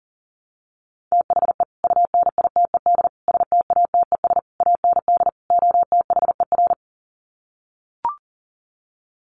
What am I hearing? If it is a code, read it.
Morse code, heard as a keyed tone, about 40 wpm, 700 Hz. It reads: THE UNITED STATES AND OTHER
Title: Sets of 5 Words Encore 40wpm